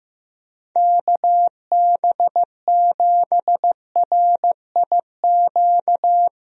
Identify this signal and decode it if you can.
Morse code, heard as a keyed tone, about 15 wpm, 700 Hz. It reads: KB7RIQ
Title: USA Call Signs - 6 Characters - Repeated 3x - 15wpm